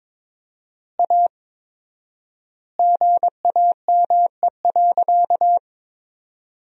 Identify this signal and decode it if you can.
Morse code, heard as a keyed tone, about 22 wpm, 700 Hz. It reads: A GAME.
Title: 4x Word-Spacing - Sentences from Top 500 Words - 22wpm